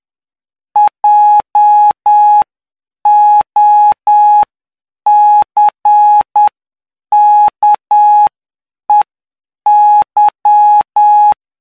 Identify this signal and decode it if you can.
Morse code, heard as a keyed tone, about 9 wpm, spaced as for 7 wpm, 840 Hz. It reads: JOCKEY
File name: cw / py2lcd / 12-PY2LCD-Li��o 21